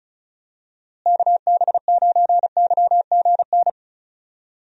Morse code, heard as a keyed tone, about 36 wpm, 700 Hz. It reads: KB9YGN
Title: USA Call Signs - 6 Characters - Repeated 3x - 35wpm